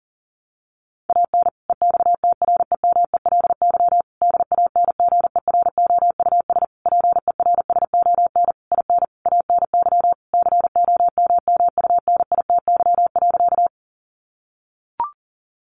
Morse code, heard as a keyed tone, about 40 wpm, 700 Hz. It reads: AN EXTREMELY DANGEROUS PERSON IN ANY COMMUNITY.
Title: Sentences from Top 1200 Words 40wpm